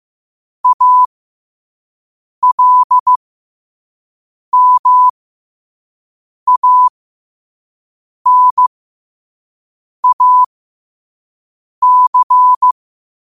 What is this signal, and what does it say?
Morse code, heard as a keyed tone, about 15 wpm, spaced as for 5 wpm, 1000 Hz. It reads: ALMANAC
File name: qotdfiles / 5wpm